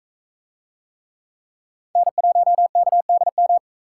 Morse code, heard as a keyed tone, about 44 wpm, 700 Hz. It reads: N1KDM